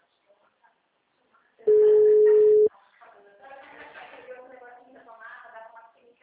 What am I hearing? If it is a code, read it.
Ringback tone (Europe, 425 Hz), with speech.
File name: out- -2006- - - .959